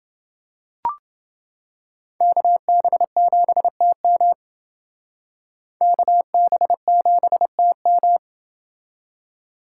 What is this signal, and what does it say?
Morse code, heard as a keyed tone, about 28 wpm, 700 Hz. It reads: KB7TM KB7TM